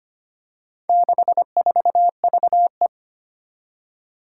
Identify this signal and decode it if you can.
Morse code, heard as a keyed tone, about 25 wpm, 700 Hz. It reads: 64VE
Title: Four Challenging Letter-Number Combinations 25wpm